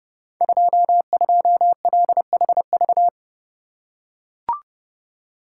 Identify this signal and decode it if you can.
Morse code, heard as a keyed tone, about 31 wpm, 700 Hz. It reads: 22LHV
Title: Code Groups - 5 Letter:Numbers - 30wpm